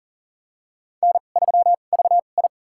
Morse code, heard as a keyed tone, about 43 wpm, spaced as for 30 wpm, 690 Hz. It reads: N3VI
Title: USA Call Signs - 4 Characters - 2x Character Spacing - 40wpm